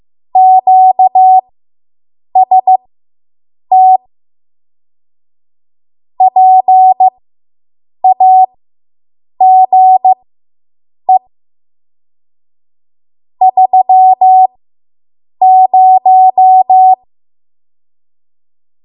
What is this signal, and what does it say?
Morse code, heard as a keyed tone, about 15 wpm, spaced as for 7 wpm, 750 Hz. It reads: QST PAGE 30